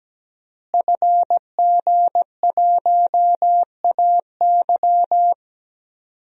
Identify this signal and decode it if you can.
Morse code, heard as a keyed tone, about 17 wpm, 700 Hz. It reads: FG1AY
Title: DXCC-100 Call Signs - 5 Characters - Repeated 3x - 17wpm